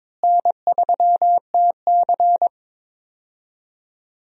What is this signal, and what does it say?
Morse code, heard as a keyed tone, about 22 wpm, 700 Hz. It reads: N3TC
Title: USA Call Signs - 4 Characters - Repeated 3x - 22wpm